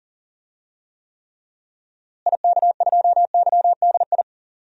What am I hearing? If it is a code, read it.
Morse code, heard as a keyed tone, about 37 wpm, 700 Hz. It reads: IK2YDI
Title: DXCC-100 Call Signs - 6 Characters - Repeated 3x - 40wpm